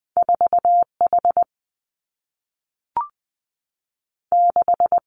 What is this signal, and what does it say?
Morse code, heard as a keyed tone, about 19 wpm, 700 Hz. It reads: 4 H 6
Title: Two Challenging Letter-Number Combinations 20wpm